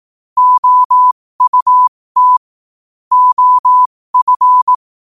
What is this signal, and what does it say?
Morse code, heard as a keyed tone, about 18 wpm, spaced as for 15 wpm, 1000 Hz. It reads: OUT OF
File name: qotdfiles / 15wpm